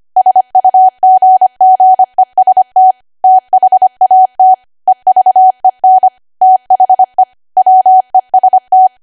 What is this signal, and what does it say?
Morse code, heard as a keyed tone, about 25 wpm, 750 Hz. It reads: SUGGEST THAT EVEN THE WEST
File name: cwa / W1AW-Level3 / 25WPM